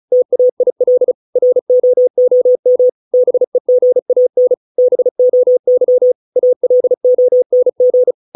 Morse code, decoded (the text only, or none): TAIL ROOM BEGAN BOY ALONG